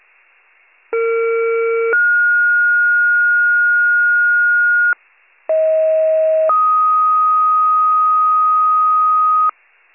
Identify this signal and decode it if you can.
Two-tone page, heard as a one-second tone, then a three-second tone, about 460 then 1430 Hz, about 640 then 1250 Hz, with noise. It is heard from 0.9 to 9.5 s.